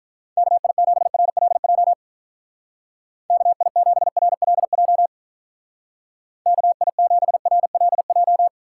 Morse code, heard as a keyed tone, about 44 wpm, 700 Hz. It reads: KI7RLJ KI7RLJ KI7RLJ
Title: Speed Racing - USA 2x3 Call Signs 35wpm